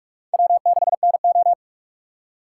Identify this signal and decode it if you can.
Morse code, heard as a keyed tone, about 46 wpm, 700 Hz. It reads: W6NO
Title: USA Call Signs - 6x Character Spacing - 45wpm